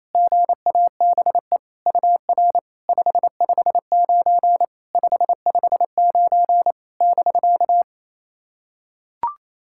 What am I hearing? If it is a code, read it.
Morse code, heard as a keyed tone, about 29 wpm, 700 Hz. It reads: GABE UR 559 559 <BK>
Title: SOTA QSO Practice 28wpm